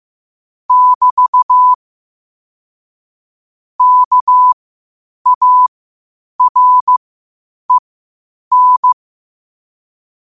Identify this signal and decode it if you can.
Morse code, heard as a keyed tone, about 15 wpm, spaced as for 8 wpm, 1000 Hz. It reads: <BT> KAREN
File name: qotdfiles / 8wpm